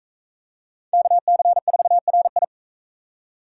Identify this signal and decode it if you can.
Morse code, heard as a keyed tone, about 38 wpm, 700 Hz. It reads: KC4RI